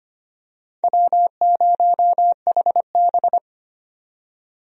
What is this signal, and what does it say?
Morse code, heard as a keyed tone, about 25 wpm, 700 Hz. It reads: W0HB
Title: USA Call Signs - 4 Characters - 25wpm